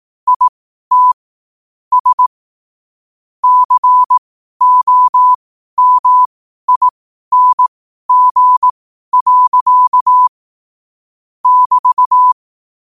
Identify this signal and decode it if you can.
Morse code, heard as a keyed tone, about 18 wpm, spaced as for 12 wpm, 1000 Hz. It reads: IT S COMING. <BT>
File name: qotdfiles / 12wpm